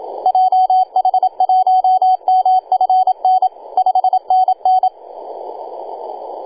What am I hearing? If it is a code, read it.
Morse code, heard as a keyed tone, about 27 wpm, 750 Hz, with radio noise. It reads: JH1MFN 5NN
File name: e6gg-18-083-cw